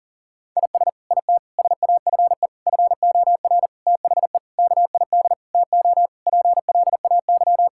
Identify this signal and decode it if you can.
Morse code, heard as a keyed tone, about 40 wpm, 700 Hz. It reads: IS IT SAFE FOR THE KID TO PLAY
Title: Sentences from Top 1600 Words 40wpm